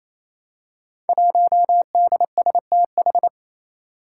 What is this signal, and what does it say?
Morse code, heard as a keyed tone, about 27 wpm, 700 Hz. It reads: 1DSTH
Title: Five Challenging Letter-Number Combinations 28wpm